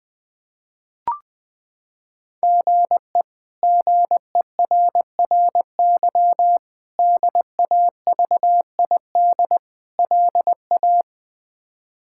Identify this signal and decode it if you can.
Morse code, heard as a keyed tone, about 21 wpm, 700 Hz. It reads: GE GERRY DAVID LA
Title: K1USN SST QSO Practice - 20wpm